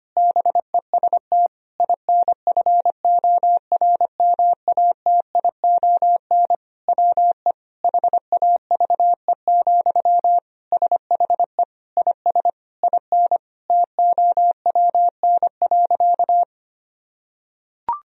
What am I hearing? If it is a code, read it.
Morse code, heard as a keyed tone, about 25 wpm, 700 Hz. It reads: BEST INFORMATION WE HAVE, SHE IS IN TOWN.
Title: Sentences from Top 1000 Words - Review Part 4 - 25wpm